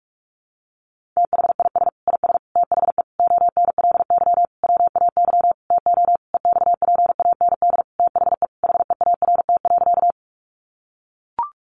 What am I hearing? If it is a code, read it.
Morse code, heard as a keyed tone, about 44 wpm, 700 Hz. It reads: THIS IS THE ONLY WAY TO EXPAND THE HEART.
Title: Sentences from Top 1400 Words 45wpm